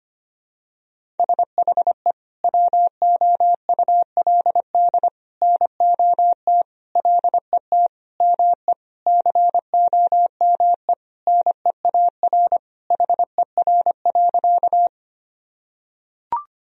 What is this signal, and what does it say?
Morse code, heard as a keyed tone, about 25 wpm, 700 Hz. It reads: SHE WOULD NOT LET ME COME NEAR HER.
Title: Sentences from Top 1000 Words - Review Part 1 - 25wpm